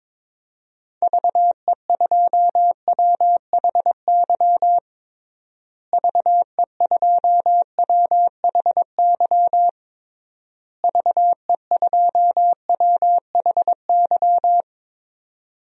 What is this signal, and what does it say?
Morse code, heard as a keyed tone, about 22 wpm, 700 Hz. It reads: VE2WHY VE2WHY VE2WHY
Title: DXCC-100 Call Signs - 6 Characters - Repeated 3x - 22wpm